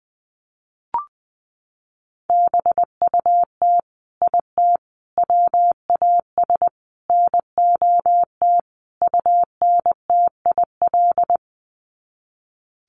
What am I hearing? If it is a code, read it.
Morse code, heard as a keyed tone, about 20 wpm, 700 Hz. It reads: BUT IT WAS NOT UNTIL
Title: Sets of 5 Words Encore2 20wpm